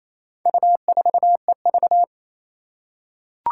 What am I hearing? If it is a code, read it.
Morse code, heard as a keyed tone, about 27 wpm, 700 Hz. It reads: U4EV
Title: Four Challenging Letter-Number Combinations 28wpm